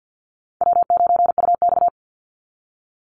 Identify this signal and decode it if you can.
Morse code, heard as a keyed tone, about 49 wpm, 700 Hz. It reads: W9UX
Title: NA SOTA Chaser Callsigns 50wpm